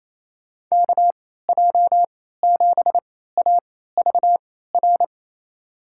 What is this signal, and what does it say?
Morse code, heard as a keyed tone, about 28 wpm, spaced as for 16 wpm, 700 Hz. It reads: KJ7AVR
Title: USA Call Signs - 6 Characters - 3x Character Spacing - 28wpm